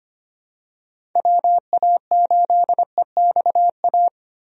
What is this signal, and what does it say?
Morse code, heard as a keyed tone, about 25 wpm, 700 Hz. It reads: WA8EXA
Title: USA Call Signs - 6 Characters - Repeated 2x - 25wpm